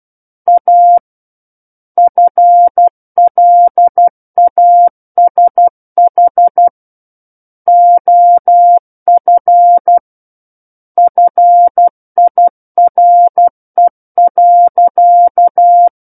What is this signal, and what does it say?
Morse code, heard as a keyed tone, about 12 wpm, 700 Hz. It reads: A FLASH OF FIRE.
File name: Book One / 03-Chapter 3 - 12 WPM